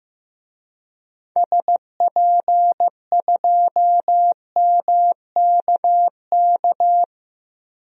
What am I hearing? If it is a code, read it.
Morse code, heard as a keyed tone, about 15 wpm, 700 Hz. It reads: SP2MKK
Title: DXCC-100 Call Signs - 6 Characters - Repeated 3x - 15wpm